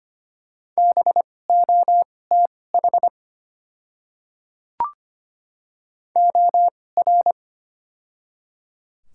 Morse code, heard as a keyed tone, about 25 wpm, spaced as for 18 wpm, 690 Hz. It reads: BOTH OR
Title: 2x Character Spacing - Top 300 Words - 25wpm